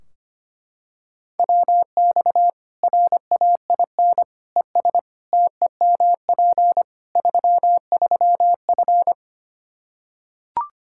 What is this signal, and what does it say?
Morse code, heard as a keyed tone, about 25 wpm, 700 Hz. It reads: WX RAIN ES TEMP 33F
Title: Practice QSO - Exchange 2 - Info - 25wpm